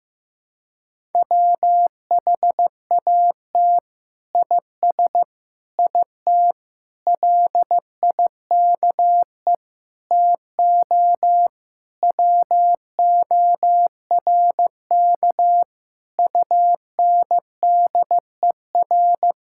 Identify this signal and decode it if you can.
Morse code, heard as a keyed tone, about 15 wpm, 700 Hz. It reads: WHAT IS IT LIKE TO WORK UNDER
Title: Sentences from Top 1000 Words - Review Part 6 - 15wpm